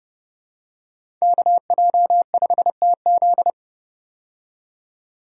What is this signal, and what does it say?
Morse code, heard as a keyed tone, about 32 wpm, 700 Hz. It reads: KJ5TZ
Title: USA Call Signs - 5 Characters - 30wpm